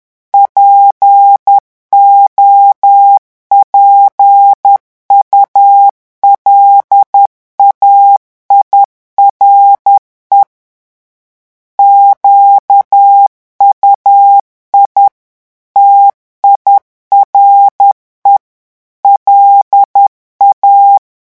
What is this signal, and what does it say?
Morse code, heard as a keyed tone, about 10 wpm, 800 Hz. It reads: POPULAIRE QUI TIRE LA
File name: morse mp3 / revision francais 10 mots minute